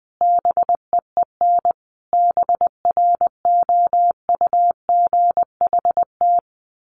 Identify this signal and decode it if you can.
Morse code, heard as a keyed tone, about 20 wpm, 700 Hz. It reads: BEEN BROUGHT
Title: Sentences from Top 1500 Words 20wpm